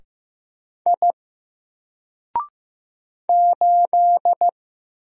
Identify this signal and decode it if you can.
Morse code, heard as a keyed tone, about 15 wpm, 700 Hz. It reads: I 8
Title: Single Character (Letter:Number:Punctuation:Prosign) 15wpm